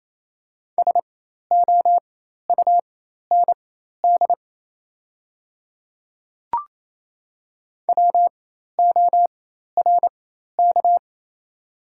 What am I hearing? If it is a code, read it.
Morse code, heard as a keyed tone, about 27 wpm, spaced as for 13 wpm, 690 Hz. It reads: SOUND WORK
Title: 4x Character Spacing - Top 300 Words - 28wpm